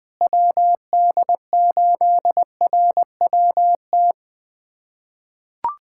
Morse code, heard as a keyed tone, about 20 wpm, 700 Hz. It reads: WD8RWT
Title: USA Callsign DE Callsign 20wpm